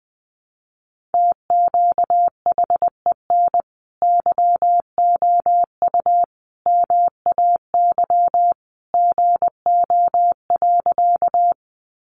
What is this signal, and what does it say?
Morse code, heard as a keyed tone, about 20 wpm, 700 Hz. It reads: TQHEN YOU MAY GO.
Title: Sentences from Top 100 Words 20wpm